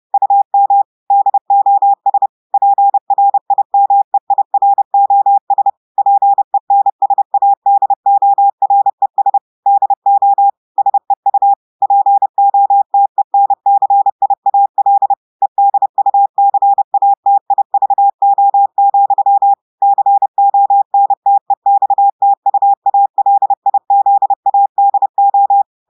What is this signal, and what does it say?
Morse code, heard as a keyed tone, about 30 wpm, 800 Hz. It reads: UM DOS PRIMEIROS PENSADORES DO SEU POTENCIAL EDUCATIVO, CONTEXTUALIZADO
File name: audios / portugais / portugais texte en clair 1 vitesse 30 mots minutes